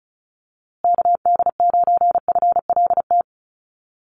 Morse code, heard as a keyed tone, about 34 wpm, 700 Hz. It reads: KD9FLT
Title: USA Call Signs Repeated 3x 35wpm